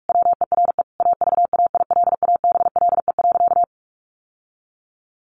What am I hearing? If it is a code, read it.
Morse code, heard as a keyed tone, about 44 wpm, 700 Hz. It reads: WERE AVAILABLE.
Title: Sentences from Top 1100 Words 45wpm